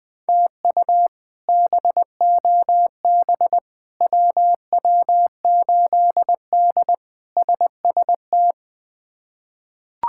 Morse code, heard as a keyed tone, about 21 wpm, 700 Hz. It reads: TU BOB WW8D SST
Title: K1USN SST QSO Practice - 20wpm